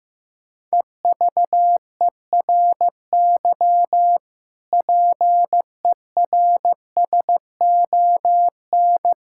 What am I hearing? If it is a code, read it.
Morse code, heard as a keyed tone, about 16 wpm, 700 Hz. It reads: EVERY PERSON
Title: Sentences from Top 1000 Words - Review Part 5 - 15wpm